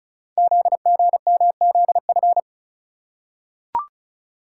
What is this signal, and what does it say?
Morse code, heard as a keyed tone, about 36 wpm, 690 Hz. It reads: ZGMZF
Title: Code Groups - 5 Letters - 35wpm